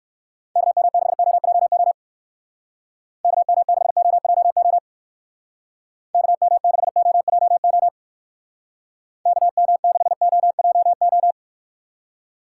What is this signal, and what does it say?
Morse code, heard as a keyed tone, about 56 wpm, 700 Hz. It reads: KM6OJO KM6OJO KM6OJO KM6OJO
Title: Speed Racing - USA 2x3 Call Signs 45wpm